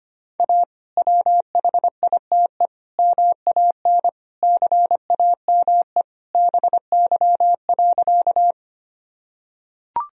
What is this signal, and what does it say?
Morse code, heard as a keyed tone, about 25 wpm, 700 Hz. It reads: A WHITE MAN CAME BY.
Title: Sentences from Top 1000 Words - Review Part 1 - 25wpm